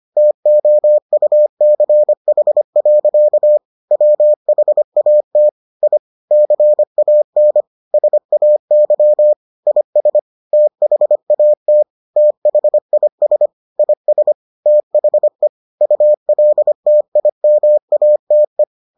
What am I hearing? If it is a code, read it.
Morse code, heard as a keyed tone, about 25 wpm, 600 Hz. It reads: TOUCH. WHAT I CAN SAY IS THAT THIS IS THE ULTIMATE